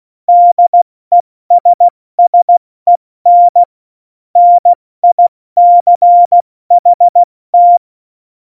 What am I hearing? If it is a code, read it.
Morse code, heard as a keyed tone, about 16 wpm, 700 Hz. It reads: DESSEN NICHT
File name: Morsekurs / Witze / Witz 23. Juli 2019, 70 F80